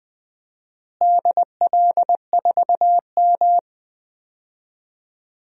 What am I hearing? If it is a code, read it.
Morse code, heard as a keyed tone, about 20 wpm, 700 Hz. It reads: DL4M
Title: DXCC-100 Call Signs - 4 Characters - Repeated 2x - 20wpm